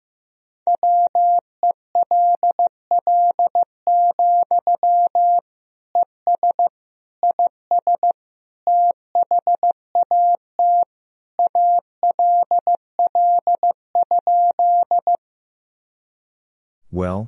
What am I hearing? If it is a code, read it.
Morse code, heard as a keyed tone, about 16 wpm, 700 Hz. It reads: WELL, ES IS THAT ALL?
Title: Sentences from Top 200 Words and Abbreviations 15wpm